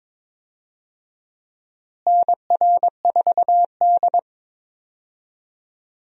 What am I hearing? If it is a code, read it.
Morse code, heard as a keyed tone, about 22 wpm, 700 Hz. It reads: NR4D
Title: USA Call Signs - 4 Characters - Repeated 2x - 22wpm